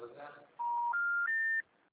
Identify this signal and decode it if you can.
Special information tone, with speech.
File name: out- -2010- - - .29910